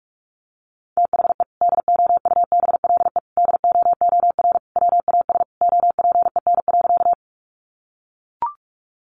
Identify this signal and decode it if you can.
Morse code, heard as a keyed tone, about 44 wpm, 700 Hz. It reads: THE DOUBLE DOOR WAS OPEN.
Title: Sentences from Top 1400 Words 45wpm